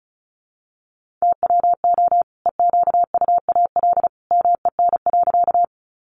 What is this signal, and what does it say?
Morse code, heard as a keyed tone, about 34 wpm, 700 Hz. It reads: TWO EQUAL MEN.
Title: Sentences from Top 1400 Words 35wpm